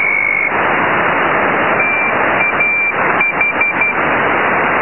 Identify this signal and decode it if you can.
Morse code, heard as a keyed tone, about 12 wpm, 2350 Hz. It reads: TAH
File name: LZW 4217.8 0408